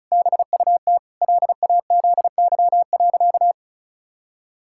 Morse code, heard as a keyed tone, about 34 wpm, 700 Hz. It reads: BUT LAZY.
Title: Pangram Challenge v2 - 35wpm